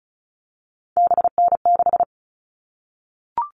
Morse code, heard as a keyed tone, about 34 wpm, 700 Hz. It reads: BN6
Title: All Cut Numbers - 3-Digit 35wpm